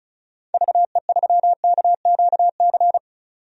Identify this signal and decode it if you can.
Morse code, heard as a keyed tone, about 36 wpm, 700 Hz. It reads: VE3KQC
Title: DXCC-100 Call Signs - 6 Characters - 35wpm